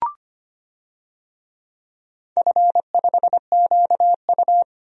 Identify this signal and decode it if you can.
Morse code, heard as a keyed tone, about 25 wpm, 700 Hz. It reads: F5QU
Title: DXCC-100 Call Signs - 4 Characters - 25wpm